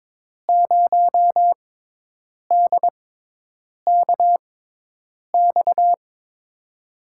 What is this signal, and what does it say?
Morse code, heard as a keyed tone, about 22 wpm, spaced as for 8 wpm, 700 Hz. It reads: 0DKX
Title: USA Call Signs - 6 Characters - 6x Character Spacing - 22wpm